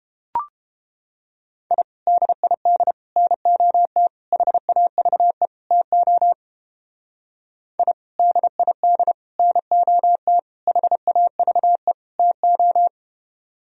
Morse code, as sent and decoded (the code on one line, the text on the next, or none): .. / -.. .. -.. / -. --- - / .... .- ...- . / - --- / .. / -.. .. -.. / -. --- - / .... .- ...- . / - ---
I DID NOT HAVE TO I DID NOT HAVE TO